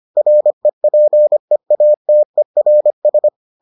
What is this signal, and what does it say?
Morse code, heard as a keyed tone, about 25 wpm, 600 Hz. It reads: REPEATERS